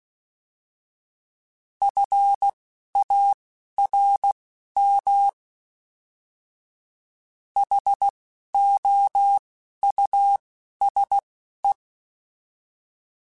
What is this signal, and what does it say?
Morse code, heard as a keyed tone, about 16 wpm, spaced as for 11 wpm, 800 Hz. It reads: FARM HOUSE